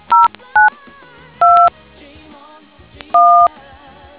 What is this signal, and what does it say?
DTMF, heard as telephone touch tones, tones of about 230 ms. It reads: 0921